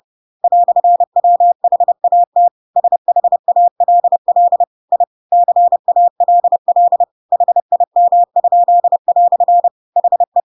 Morse code, heard as a keyed tone, about 30 wpm, 700 Hz. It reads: "WHAT SHALL I CALL HIM?" HE
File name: Pinocchio / Pinocchio ch3 30 30 ik1hge